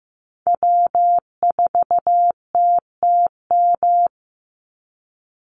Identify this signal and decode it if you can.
Morse code, heard as a keyed tone, about 15 wpm, 700 Hz. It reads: W4TTM
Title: Call Signs Repeated 3x - 15wpm